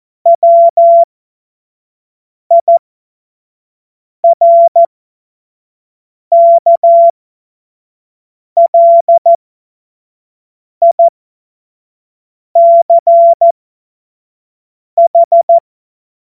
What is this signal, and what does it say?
Morse code, heard as a keyed tone, about 14 wpm, spaced as for 5 wpm, 680 Hz. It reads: WIRKLICH
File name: Morsekurs / Witze / Witz 27. Dezember 2022 25 F70